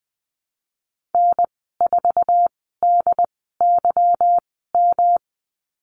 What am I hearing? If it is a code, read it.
Morse code, heard as a keyed tone, about 20 wpm, spaced as for 15 wpm, 700 Hz. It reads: N4DYM
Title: 147 Lesson 24g - Letter y - Call Signs 20wpm